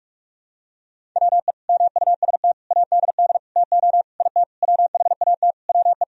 Morse code, heard as a keyed tone, about 44 wpm, 700 Hz. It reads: WE MUST ADD TO IT WHAT WE
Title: Sentences from Top 1000 Words - Review Part 1 - 45wpm